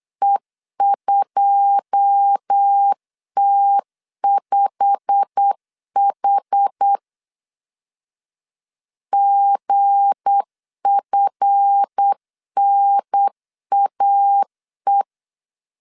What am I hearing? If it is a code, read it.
Morse code, heard as a keyed tone, about 8 wpm, 800 Hz. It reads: E2T5H GFNAE
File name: LEZIONI / 9 2 8wpm 3437